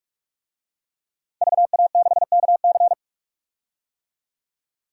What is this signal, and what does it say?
Morse code, heard as a keyed tone, about 42 wpm, 700 Hz. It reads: VA6KC
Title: DXCC-100 Call Signs - 5 Characters - 45wpm